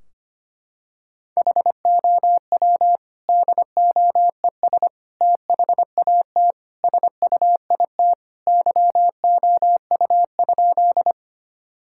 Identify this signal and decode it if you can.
Morse code, heard as a keyed tone, about 25 wpm, 700 Hz. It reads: HOW DOES THAT SUIT YOU?
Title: Sentences from Top 1600 Words 25wpm